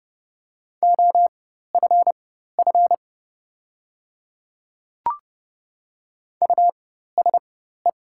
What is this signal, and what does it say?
Morse code, heard as a keyed tone, about 30 wpm, spaced as for 14 wpm, 690 Hz. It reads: OFF USE
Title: 4x Character Spacing - Top 200 Words - 30wpm